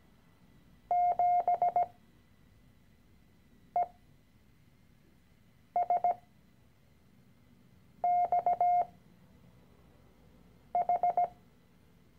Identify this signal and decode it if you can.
Morse code, heard as a keyed tone, about 17 wpm, spaced as for 4 wpm, 710 Hz. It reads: 7ESXH
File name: CW-Online-Kurs / Anfänger / Lektion 013